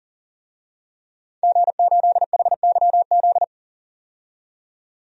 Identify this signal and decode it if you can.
Morse code, heard as a keyed tone, about 37 wpm, 700 Hz. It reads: G8HYZ
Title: DXCC-100 Call Signs - 5 Characters - 40wpm